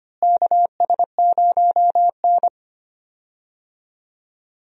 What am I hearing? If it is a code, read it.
Morse code, heard as a keyed tone, about 25 wpm, 700 Hz. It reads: KS0N
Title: USA Call Signs - 4 Characters - Repeated 2x - 25wpm